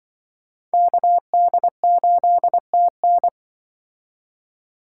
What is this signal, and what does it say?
Morse code, heard as a keyed tone, about 24 wpm, 700 Hz. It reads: KD8TN
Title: Speed Racing - USA Call Signs 20wpm